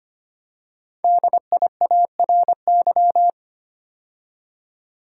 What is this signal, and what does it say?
Morse code, heard as a keyed tone, about 25 wpm, 700 Hz. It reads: DIARY
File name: 5 Letter Words - First 3 Letters Given - 25wpm